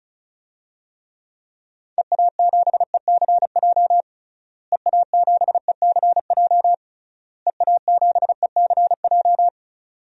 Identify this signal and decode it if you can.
Morse code, heard as a keyed tone, about 34 wpm, 700 Hz. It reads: EA7ECJ EA7ECJ EA7ECJ
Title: DXCC-100 Call Signs - 6 Characters - Repeated 3x - 35wpm